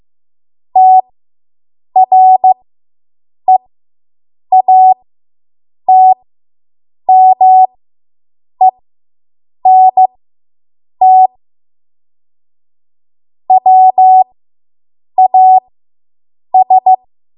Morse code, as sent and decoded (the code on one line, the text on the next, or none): - .-. .- - -- . -. - / .-- .- ...
TREATMENT WAS